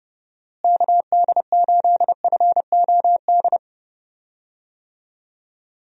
Morse code, heard as a keyed tone, about 31 wpm, 700 Hz. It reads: KD8FOD
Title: USA Call Signs - 6 Characters - Repeated 2x - 30wpm